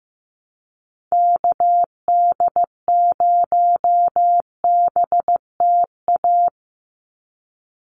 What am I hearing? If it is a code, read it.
Morse code, heard as a keyed tone, about 15 wpm, 700 Hz. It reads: KD0BTA
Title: USA Call Signs 15wpm